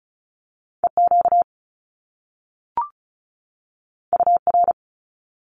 Morse code, heard as a keyed tone, about 36 wpm, 700 Hz. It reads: E Q U R
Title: Top Two Letter Combinations in Words 35wpm